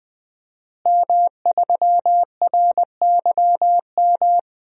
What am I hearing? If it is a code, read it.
Morse code, heard as a keyed tone, about 20 wpm, 700 Hz. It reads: M3RYM
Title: DXCC-100 Call Signs - 5 Characters - Repeated 2x - 20wpm